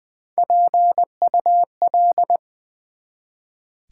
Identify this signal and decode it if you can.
Morse code, heard as a keyed tone, about 20 wpm, 700 Hz. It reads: PUL
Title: Build A Word - 1 Letter At A Time - 4-Letter Words - 20wpm